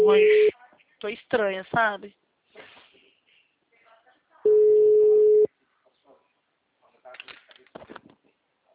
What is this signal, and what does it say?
Ringback tone (Europe, 425 Hz), with speech.